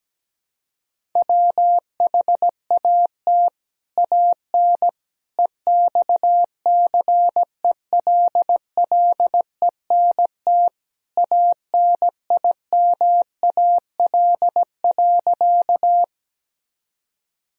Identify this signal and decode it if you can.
Morse code, heard as a keyed tone, about 17 wpm, 700 Hz. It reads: WHAT AN EXCELLENT ANIMAL.
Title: Sentences from Top 1000 Words - Review Part 1 - 17wpm